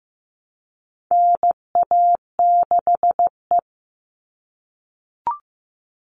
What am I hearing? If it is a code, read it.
Morse code, heard as a keyed tone, about 15 wpm, 700 Hz. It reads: NA6E
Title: All Cut Numbers - 4-Digit 15wpm